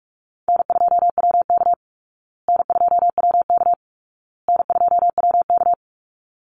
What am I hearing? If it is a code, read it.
Morse code, heard as a keyed tone, about 44 wpm, 700 Hz. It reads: N2WK N2WK N2WK
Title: Easy Call Signs Repeated 3x - 45wpm